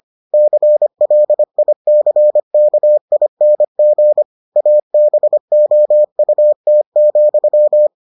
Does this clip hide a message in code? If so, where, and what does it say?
Yes: Morse code from 0.3 to 7.9 s, heard as a keyed tone, about 25 wpm, 600 Hz. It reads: CLICKING ABOUT,